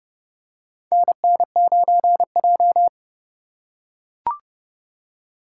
Morse code, heard as a keyed tone, about 28 wpm, 700 Hz. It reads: NN9J
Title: USA Call Signs - 4 Characters - 4x Character Spacing - 30wpm